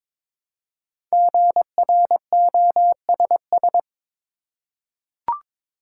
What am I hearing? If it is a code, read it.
Morse code, heard as a keyed tone, about 22 wpm, 700 Hz. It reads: GROSS
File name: 5 Letter Words - First 3 Letters Given - 22wpm